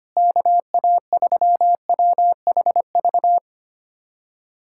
Morse code, heard as a keyed tone, about 25 wpm, 700 Hz. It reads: KA3WHV
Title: USA Callsign DE Callsign - 6 Characters - 25wpm